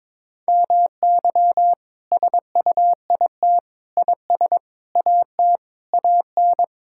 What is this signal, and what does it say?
Morse code, heard as a keyed tone, about 22 wpm, 700 Hz. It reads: MY SUIT IS AT AN
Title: Sentences from Top 1600 Words 22wpm